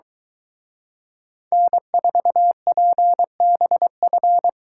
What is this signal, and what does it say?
Morse code, heard as a keyed tone, about 23 wpm, 700 Hz. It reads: N4PBF